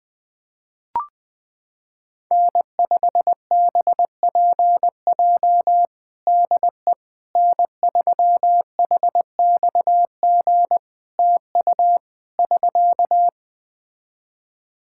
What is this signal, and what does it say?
Morse code, heard as a keyed tone, about 21 wpm, 700 Hz. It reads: N5BPJ DE N3HXG TU <SK>
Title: K1USN SST QSO Practice 20wpm